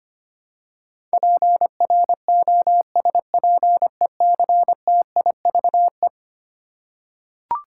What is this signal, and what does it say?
Morse code, heard as a keyed tone, about 25 wpm, 700 Hz. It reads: PROSPECTIVE
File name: practice / 25wpm / Eleven Letter Words - 25wpm